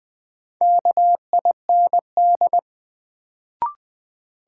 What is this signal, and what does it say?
Morse code, heard as a keyed tone, about 20 wpm, 700 Hz. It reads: KIND
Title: Sets of 2 Words - New Word Drill - 20wpm